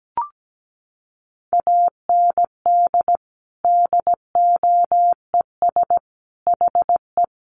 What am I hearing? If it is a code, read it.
Morse code, heard as a keyed tone, about 17 wpm, 700 Hz. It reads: AND DOES HE
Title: Sentences from Top 1500 Words 17wpm